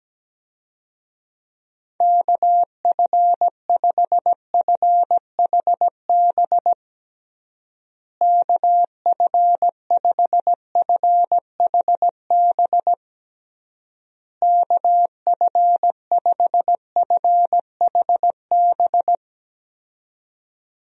Morse code, heard as a keyed tone, about 17 wpm, 700 Hz. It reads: KF5FHB KF5FHB KF5FHB
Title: USA Call Signs - 6 Characters - Repeated 3x - 17wpm